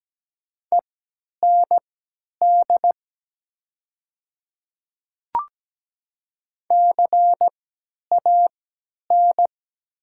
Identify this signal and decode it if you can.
Morse code, heard as a keyed tone, about 17 wpm, spaced as for 10 wpm, 700 Hz. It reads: END CAN
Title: 3x Character Spacing - Top 200 Words - 17wpm